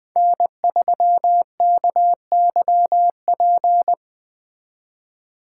Morse code, heard as a keyed tone, about 20 wpm, 700 Hz. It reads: N3KYP
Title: Speed Racing - USA 1x3 Call Signs 17wpm